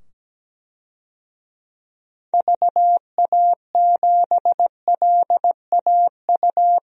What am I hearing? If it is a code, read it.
Morse code, heard as a keyed tone, about 17 wpm, 700 Hz. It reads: VA7LAU
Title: DXCC-100 Call Signs - 6 Characters - 17wpm